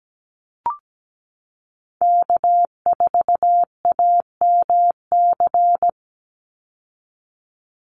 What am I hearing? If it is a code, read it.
Morse code, heard as a keyed tone, about 17 wpm, 700 Hz. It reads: K4AMC
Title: Call Signs Repeated 2x - 17wpm